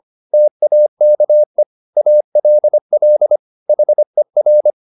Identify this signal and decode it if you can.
Morse code, heard as a keyed tone, about 25 wpm, 600 Hz. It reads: TAKE ALL HER